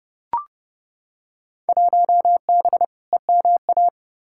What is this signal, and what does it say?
Morse code, heard as a keyed tone, about 30 wpm, 700 Hz. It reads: 1B EMA
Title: Field Day Exchanges - 30wpm